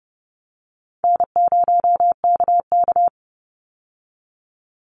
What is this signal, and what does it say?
Morse code, heard as a keyed tone, about 29 wpm, 700 Hz. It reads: N0KK
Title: Easy Call Signs Repeated 3x - 30wpm